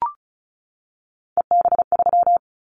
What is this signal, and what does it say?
Morse code, heard as a keyed tone, about 36 wpm, 700 Hz. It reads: EB3
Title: Callsign Prefixes for Top 25 DXCC - 35wpm